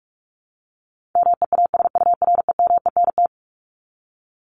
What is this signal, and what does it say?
Morse code, heard as a keyed tone, about 44 wpm, 700 Hz. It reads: MEASUREMENT
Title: Top 1751-2000 Words 45wpm